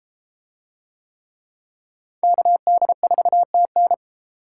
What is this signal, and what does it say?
Morse code, heard as a keyed tone, about 32 wpm, 700 Hz. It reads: KD4TN